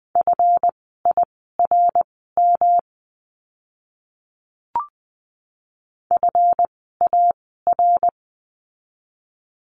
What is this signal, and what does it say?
Morse code, heard as a keyed tone, about 20 wpm, spaced as for 14 wpm, 700 Hz. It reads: FIRM FAR
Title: 139 Lesson 23f - Letter f - Words 20wpm